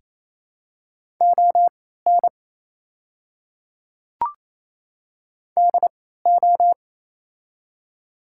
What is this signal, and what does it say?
Morse code, heard as a keyed tone, about 28 wpm, spaced as for 16 wpm, 700 Hz. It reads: ON DO
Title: 3x Character Spacing - Top 200 Words - 28wpm